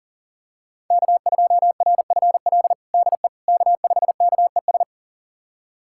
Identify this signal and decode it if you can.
Morse code, heard as a keyed tone, about 40 wpm, 700 Hz. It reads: K2RFL DE K5KES
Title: USA Callsign DE Callsign - 5 Characters - 40wpm